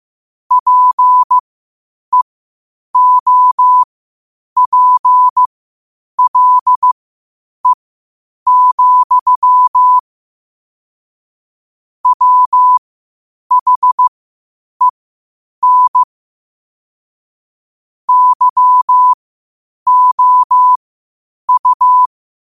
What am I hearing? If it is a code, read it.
Morse code, heard as a keyed tone, about 15 wpm, spaced as for 8 wpm, 1000 Hz. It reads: PEOPLE, WHEN YOU
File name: qotdfiles / 8wpm